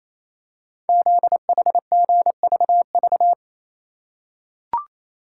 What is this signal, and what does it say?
Morse code, heard as a keyed tone, about 29 wpm, 700 Hz. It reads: ZHGVV
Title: Code Groups - 5 Letters - 28wpm